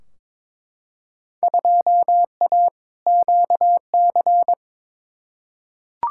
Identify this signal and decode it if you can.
Morse code, heard as a keyed tone, about 22 wpm, 700 Hz. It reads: 2A QC
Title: Field Day Exchanges - 22wpm